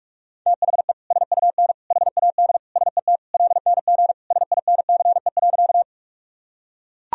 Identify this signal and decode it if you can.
Morse code, heard as a keyed tone, about 41 wpm, 700 Hz. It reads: THE SUN HAD SET LNG SINCE.
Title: Sentences from Top 300 Words and Abbreviations 45wpm